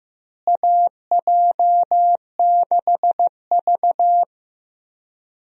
Morse code, heard as a keyed tone, about 15 wpm, 700 Hz. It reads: AJ6V
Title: USA Call Signs - 4 Characters - Repeated 3x - 15wpm